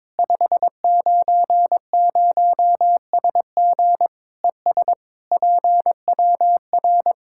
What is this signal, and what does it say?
Morse code, heard as a keyed tone, about 22 wpm, 700 Hz. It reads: 590SG ES PWR
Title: Practice QSO - Exchange 2 - Info - 22wpm